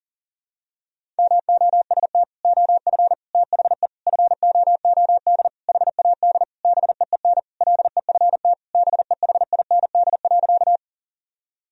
Morse code, heard as a keyed tone, about 38 wpm, 700 Hz. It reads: MOST OF THE FOOD HAD BEEN LEFT BEHIND.
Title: Sentences from Top 1000 Words - Review Part 1 - 40wpm